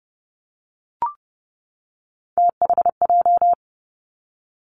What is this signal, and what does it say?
Morse code, heard as a keyed tone, about 32 wpm, 700 Hz. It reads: THJ
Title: Three Challenging Letter-Number Combinations 30wpm